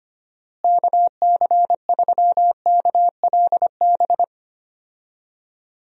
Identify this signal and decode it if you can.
Morse code, heard as a keyed tone, about 25 wpm, 700 Hz. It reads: KC3KLB
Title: USA Call Signs - 6 Characters - Repeated 3x - 25wpm